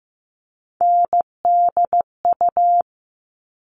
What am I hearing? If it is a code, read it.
Morse code, heard as a keyed tone, about 15 wpm, 700 Hz. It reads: NDU
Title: All Cut Numbers - 3-Digit 15wpm